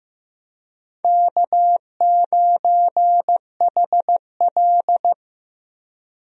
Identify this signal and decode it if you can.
Morse code, heard as a keyed tone, about 15 wpm, 700 Hz. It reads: K9HL
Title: USA Call Signs - 4 Characters - 15wpm